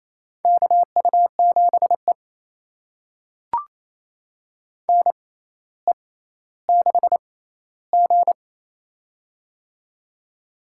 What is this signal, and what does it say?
Morse code, heard as a keyed tone, about 28 wpm, spaced as for 14 wpm, 700 Hz. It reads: KU7E NE6G
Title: USA Call Signs - 4 Characters - 6x Character Spacing - 28wpm